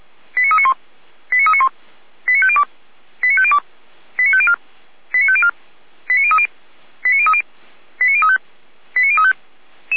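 Five-tone selective call (ZVEI), heard as quick runs of five tones, tones of about 70 ms. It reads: 89391 89391 89502 89502 89504 89504 89030 89030 89035 89035
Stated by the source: ZVEI five-tone selective call 89050, received by SDR.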